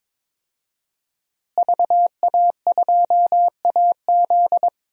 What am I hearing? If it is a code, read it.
Morse code, heard as a keyed tone, about 22 wpm, 700 Hz. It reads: VA2AZ
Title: DXCC-100 Call Signs - 5 Characters - Repeated 3x - 22wpm